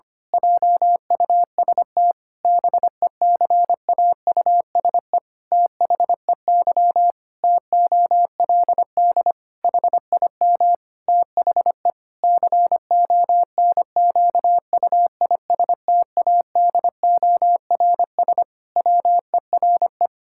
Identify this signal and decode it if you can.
Morse code, heard as a keyed tone, about 25 wpm, 700 Hz. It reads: JUST BECAUSE THEY TOLD HIM THE CONQUISTADORS WERE
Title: Pangram Challenge v2 - 25wpm